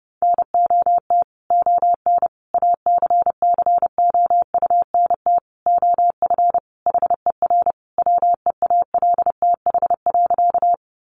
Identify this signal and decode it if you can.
Morse code, heard as a keyed tone, about 30 wpm, 700 Hz. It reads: NOT ON ACCOUNT OF HER WEALTH.
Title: Sentences from Top 1200 Words 30wpm